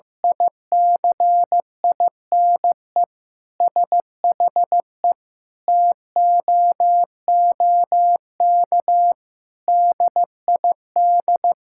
Morse code, heard as a keyed tone, about 16 wpm, 700 Hz. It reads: ICINE SHE TOOK DID
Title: Sentences from Top 1000 Words - Review Part 6 - 15wpm